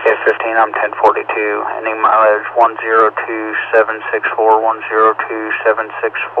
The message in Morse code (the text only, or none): none